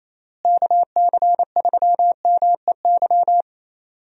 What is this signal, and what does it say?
Morse code, heard as a keyed tone, about 28 wpm, 700 Hz. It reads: KC3MEY
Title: USA Call Signs - 6 Characters - Repeated 3x - 28wpm